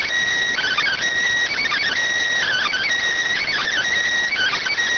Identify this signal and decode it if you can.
Emergency Alert System (SAME) tones, heard as harsh buzzing bursts. It is heard from 0.5 to 4.7 s.